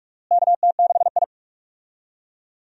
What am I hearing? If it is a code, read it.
Morse code, heard as a keyed tone, about 46 wpm, 700 Hz. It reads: KT6I
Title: USA Call Signs - 4 Characters - Repeated 3x - 45wpm